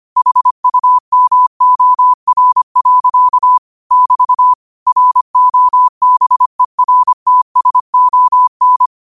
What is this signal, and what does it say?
Morse code, heard as a keyed tone, about 25 wpm, 1000 Hz. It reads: HUMOR. <BT> ROBERTSON
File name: qotdfiles / 25wpm